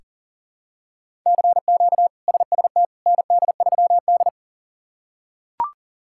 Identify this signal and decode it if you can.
Morse code, heard as a keyed tone, about 40 wpm, 700 Hz. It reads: CQ SST ND3D
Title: K1USN SST QSO Practice - 40wpm